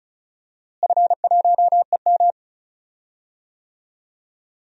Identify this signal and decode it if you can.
Morse code, heard as a keyed tone, about 36 wpm, 700 Hz. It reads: F1EM
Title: DXCC-100 Call Signs - 4 Characters - Repeated 3x - 35wpm